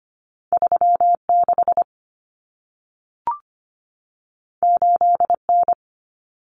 Morse code, heard as a keyed tone, about 25 wpm, 700 Hz. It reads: 36 8N
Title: Common Cut Numbers - 2-Digit 25wpm